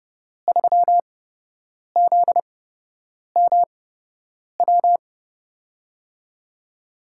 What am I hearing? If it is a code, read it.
Morse code, heard as a keyed tone, about 30 wpm, spaced as for 8 wpm, 700 Hz. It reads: 3ZMW
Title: USA Call Signs - 6 Characters - 8x Character Spacing - 30wpm